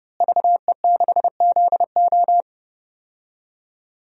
Morse code, heard as a keyed tone, about 30 wpm, 700 Hz. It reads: VE6ZO
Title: DXCC-100 Call Signs - 5 Characters - Repeated 2x - 30wpm